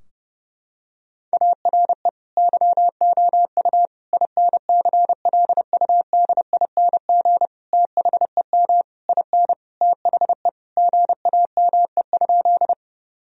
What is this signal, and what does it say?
Morse code, heard as a keyed tone, about 30 wpm, 700 Hz. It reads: ARE YOU INCLUDING THEM IN THE GAME?
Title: Sentences from Top 1000 Words - Review Part 4 - 30wpm